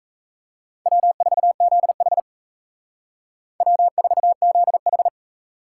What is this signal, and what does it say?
Morse code, heard as a keyed tone, about 34 wpm, 700 Hz. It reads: W4ZH W4ZH
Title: Speed Racing - USA 1x2 Call Signs 28wpm